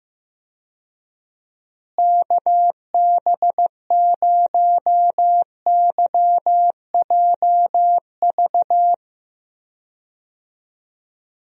Morse code, heard as a keyed tone, about 15 wpm, 700 Hz. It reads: KB0YJV
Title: USA Call Signs - 6 Characters - Repeated 2x - 15wpm